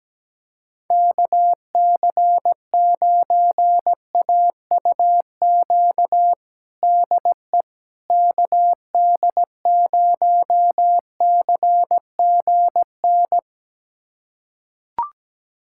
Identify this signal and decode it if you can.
Morse code, heard as a keyed tone, about 17 wpm, 700 Hz. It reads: KC9AUQ DE KD0CGN
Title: USA Callsign DE Callsign - 6 Characters - 17wpm